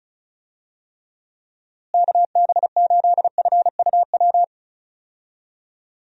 Morse code, heard as a keyed tone, about 36 wpm, 700 Hz. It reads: KB8FUW
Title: USA Call Signs - 6 Characters - Repeated 3x - 35wpm